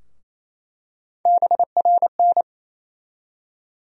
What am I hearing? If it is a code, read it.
Morse code, heard as a keyed tone, about 28 wpm, 700 Hz. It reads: BRN
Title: USA Call Sign Postfixes - 3 Letters 28wpm